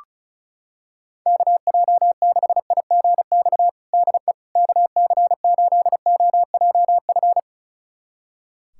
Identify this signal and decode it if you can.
Morse code, heard as a keyed tone, about 34 wpm, 700 Hz. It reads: KJ6IGX DE KC8OJF
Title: USA Callsign DE Callsign - 6 Characters - 35wpm